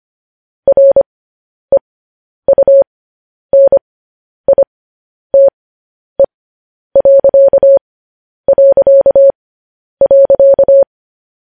Morse code, heard as a keyed tone, about 25 wpm, spaced as for 10 wpm, 550 Hz. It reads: REUNITE...
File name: podcast / News headlines-1x-25x10